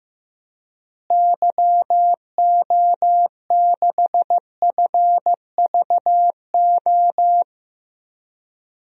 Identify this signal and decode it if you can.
Morse code, heard as a keyed tone, about 15 wpm, 700 Hz. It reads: YO6FVO
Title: DXCC-100 Call Signs - 6 Characters - Repeated 3x - 15wpm